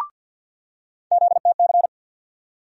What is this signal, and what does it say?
Morse code, heard as a keyed tone, about 47 wpm, 700 Hz. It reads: ZTC
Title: USA Call Sign Postfixes - 3 Letters 50wpm